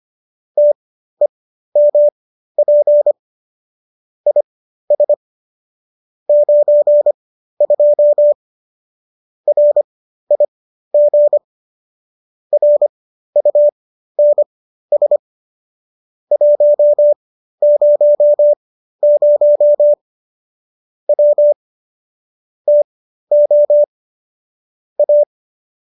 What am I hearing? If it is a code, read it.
Morse code, heard as a keyed tone, about 25 wpm, spaced as for 13 wpm, 590 Hz. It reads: TEMP IS 92 RIG RUNS 100 W TO A